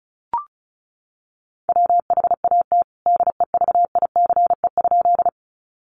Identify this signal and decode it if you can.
Morse code, heard as a keyed tone, about 34 wpm, 700 Hz. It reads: WHAT DEVICE?
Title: Sentences from Top 1500 Words 35wpm